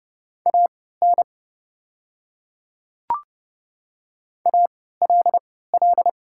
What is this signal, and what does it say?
Morse code, heard as a keyed tone, about 29 wpm, spaced as for 17 wpm, 690 Hz. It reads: AN ALL
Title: 3x Character Spacing - Top 100 Words - 30wpm